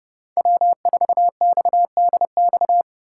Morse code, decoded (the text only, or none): W4XDX